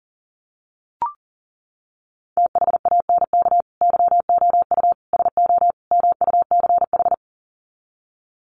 Morse code, heard as a keyed tone, about 40 wpm, 700 Hz. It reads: THANK YOU SO MUCH
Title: Sets of 4 Words 40wpm